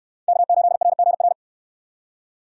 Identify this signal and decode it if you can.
Morse code, heard as a keyed tone, about 69 wpm, 700 Hz. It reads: K8RGR